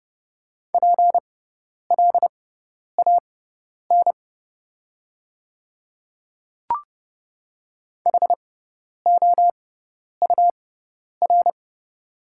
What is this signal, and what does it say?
Morse code, heard as a keyed tone, about 30 wpm, spaced as for 10 wpm, 700 Hz. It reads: PLAN HOUR
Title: 6x Character Spacing - Top 500 Words - 30wpm